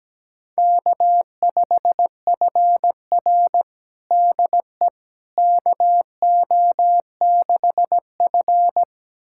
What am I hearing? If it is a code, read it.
Morse code, heard as a keyed tone, about 17 wpm, 700 Hz. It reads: K5FR DE KO6F
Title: USA Callsign DE Callsign - 4 Characters - 17wpm